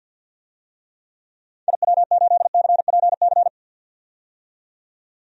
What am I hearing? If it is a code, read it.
Morse code, heard as a keyed tone, about 46 wpm, 700 Hz. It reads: IW8CPC
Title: DXCC-100 Call Signs - 6 Characters - 50wpm